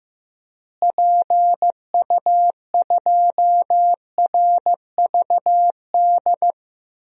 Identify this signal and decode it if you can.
Morse code, heard as a keyed tone, about 15 wpm, 700 Hz. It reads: PU2RVD
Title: DXCC-100 Call Signs - 6 Characters - Repeated 2x - 15wpm